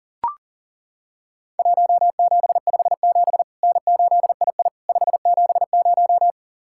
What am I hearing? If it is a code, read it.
Morse code, heard as a keyed tone, about 40 wpm, 700 Hz. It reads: 1757 N8II 570
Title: Worked All Europe DX Contest - QTC Exchange - 40wpm